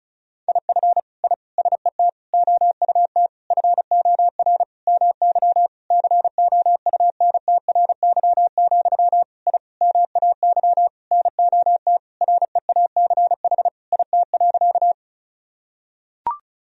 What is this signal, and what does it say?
Morse code, heard as a keyed tone, about 36 wpm, 700 Hz. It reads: IF I SET OUT FOR MY COUNTRY, I MAY NOT REACH IT.
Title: Sentences from Top 1000 Words - Review Part 2 - 35wpm